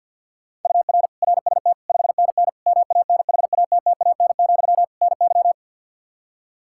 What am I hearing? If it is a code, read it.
Morse code, heard as a keyed tone, about 50 wpm, 700 Hz. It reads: UR RST 5NN MANHATTAN, NY
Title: Practice QSO - Exchange 1 - Beginning - 50wpm